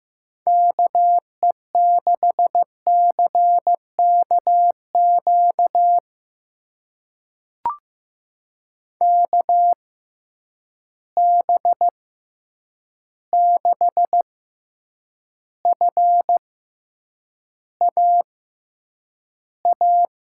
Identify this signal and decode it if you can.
Morse code, heard as a keyed tone, about 15 wpm, spaced as for 8 wpm, 700 Hz. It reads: KE6CKQ KB6FAA
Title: USA Call Signs - 6 Characters - 6x Character Spacing - 15wpm